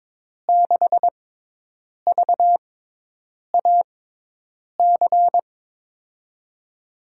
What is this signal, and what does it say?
Morse code, heard as a keyed tone, about 22 wpm, spaced as for 8 wpm, 700 Hz. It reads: 6VAC
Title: USA Call Signs - 5 Characters - 6x Character Spacing - 22wpm